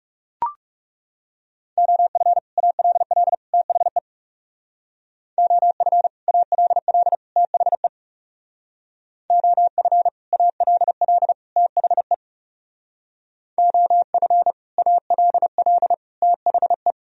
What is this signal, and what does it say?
Morse code, heard as a keyed tone, about 37 wpm, 700 Hz. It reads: OF ALL THE OF ALL THE OF ALL THE OF ALL THE
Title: Speed-Racing - Sets of 3 Words 30wpm